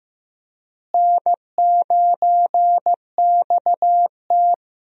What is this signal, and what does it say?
Morse code, heard as a keyed tone, about 15 wpm, 700 Hz. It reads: N9XT
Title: USA Call Signs - 4 Characters - Repeated 3x - 15wpm